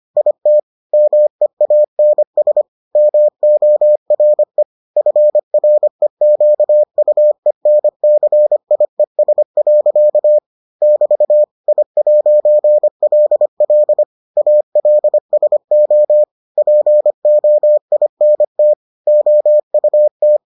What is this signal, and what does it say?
Morse code, heard as a keyed tone, about 25 wpm, 600 Hz. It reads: IT MEANS MORE FREQUENCIES. <BT> I'LL ALSO POINT OUT